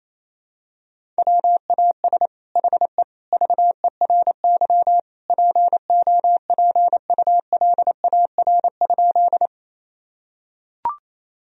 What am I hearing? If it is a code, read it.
Morse code, heard as a keyed tone, about 28 wpm, 700 Hz. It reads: WAS HE VERY POPULAR?
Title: Sentences from Top 1000 Words - Review Part 3 - 28wpm